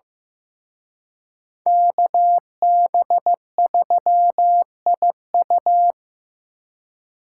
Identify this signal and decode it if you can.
Morse code, heard as a keyed tone, about 18 wpm, 700 Hz. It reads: KB3IU
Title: Speed Racing - USA Call Signs 15wpm